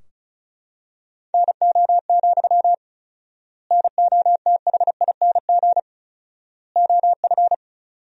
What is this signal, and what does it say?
Morse code, heard as a keyed tone, about 36 wpm, 700 Hz. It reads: NO, NOTHING OF
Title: 4x Word-Spacing - Sentences from Top 500 Words - 35wpm